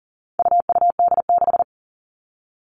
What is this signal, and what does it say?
Morse code, heard as a keyed tone, about 40 wpm, 700 Hz. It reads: UUD6
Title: All Cut Numbers - 4-Digit 40wpm